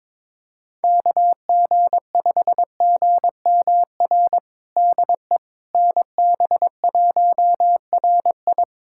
Morse code, heard as a keyed tone, about 21 wpm, 700 Hz. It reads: KG5GMR DE NB1RI
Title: USA Callsign DE Callsign 22wpm